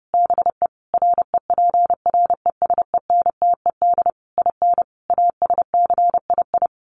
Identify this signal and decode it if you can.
Morse code, heard as a keyed tone, about 30 wpm, 700 Hz. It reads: BE REPRESENTED IN ASCII
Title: Sets of 5 Words 30wpm